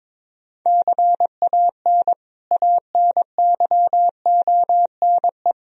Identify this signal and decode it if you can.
Morse code, heard as a keyed tone, about 22 wpm, 700 Hz. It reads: CAN ANYONE
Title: Sentences from Top 1000 Words - Review Part 4 - 22wpm